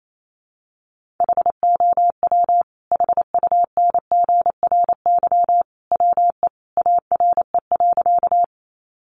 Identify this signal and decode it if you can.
Morse code, heard as a keyed tone, about 28 wpm, 700 Hz. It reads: HOW HUNGRY WE ARE.
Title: Sentences from Top 1400 Words 28wpm